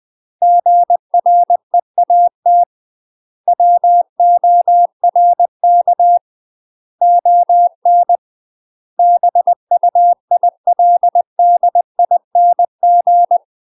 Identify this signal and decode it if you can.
Morse code, heard as a keyed tone, about 20 wpm, 700 Hz. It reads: GREAT WORK ON BUILDING
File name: UK MASTER QSOs TRAINING / 20WPM / UK MASTER QSOs 0056